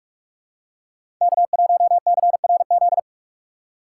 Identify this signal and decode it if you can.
Morse code, heard as a keyed tone, about 46 wpm, 690 Hz. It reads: K1CRZ
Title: USA Call Signs - 5 Characters - 4x Character Spacing - 45wpm